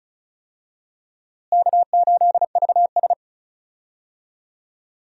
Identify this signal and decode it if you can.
Morse code, heard as a keyed tone, about 36 wpm, 700 Hz. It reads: K8VS
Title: USA Call Signs - 4 Characters - Repeated 3x - 35wpm